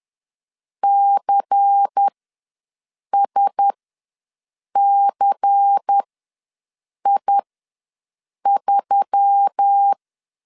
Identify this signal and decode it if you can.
Morse code, heard as a keyed tone, about 10 wpm, spaced as for 6 wpm, 800 Hz. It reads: CSCI3
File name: LEZIONI / 5 LBVC / 5 LBVC ws 10wpm 3733